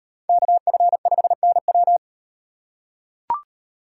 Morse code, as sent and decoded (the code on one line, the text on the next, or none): -.- ..-. ..... -. .--
KF5NW